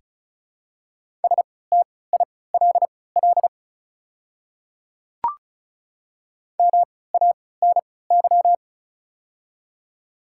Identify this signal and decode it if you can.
Morse code, heard as a keyed tone, about 36 wpm, spaced as for 20 wpm, 700 Hz. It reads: STILL MANY
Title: 3x Character Spacing - Top 300 Words - 35wpm